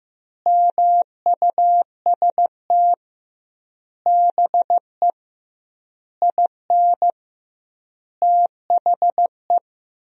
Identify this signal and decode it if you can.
Morse code, heard as a keyed tone, about 15 wpm, 700 Hz. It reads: MUST BE IN THE
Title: 2x Word-Spacing - Sentences from Top 500 Words - 15wpm